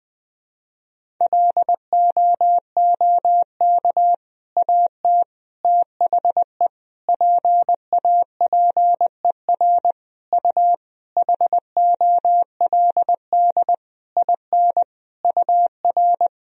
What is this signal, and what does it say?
Morse code, heard as a keyed tone, about 20 wpm, 700 Hz. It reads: LOOK AT THE PAPER U HOLD IN UR
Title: Sentences from Top 400 Words and Abbreviations 20wpm